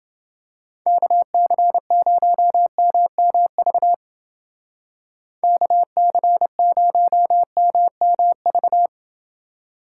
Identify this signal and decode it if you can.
Morse code, heard as a keyed tone, about 30 wpm, 700 Hz. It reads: KC0MMV KC0MMV